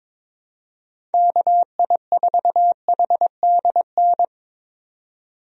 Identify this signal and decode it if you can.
Morse code, heard as a keyed tone, about 22 wpm, 700 Hz. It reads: KI4HDN
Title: USA Call Signs - 6 Characters - 22wpm